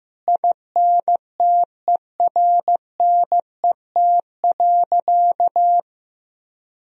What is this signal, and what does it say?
Morse code, heard as a keyed tone, about 15 wpm, 700 Hz. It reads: INTERNET.
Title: Sentences from Top 1000 Words - Review Part 4 - 15wpm